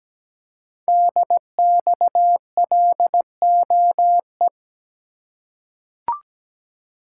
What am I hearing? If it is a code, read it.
Morse code, heard as a keyed tone, about 17 wpm, 700 Hz. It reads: DXLOE
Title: Code Groups - 5 Letters - 17wpm